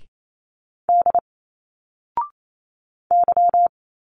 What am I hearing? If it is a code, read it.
Morse code, heard as a keyed tone, about 28 wpm, 700 Hz. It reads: D Y
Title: Single Letters 28wpm